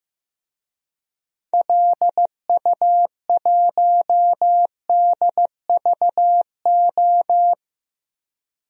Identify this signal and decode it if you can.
Morse code, heard as a keyed tone, about 15 wpm, 700 Hz. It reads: LU1DVO
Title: DXCC-100 Call Signs - 6 Characters - 15wpm